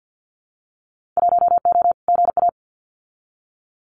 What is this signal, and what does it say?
Morse code, heard as a keyed tone, about 50 wpm, 700 Hz. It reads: 1O GA
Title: Winter Field Day Exchanges 50wpm